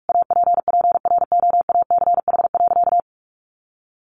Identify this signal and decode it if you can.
Morse code, heard as a keyed tone, about 44 wpm, 700 Hz. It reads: APPROACH.
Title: Sentences from Top 1400 Words 45wpm